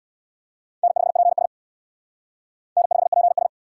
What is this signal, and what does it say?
Morse code, heard as a keyed tone, about 64 wpm, 700 Hz. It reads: N5PS N5PS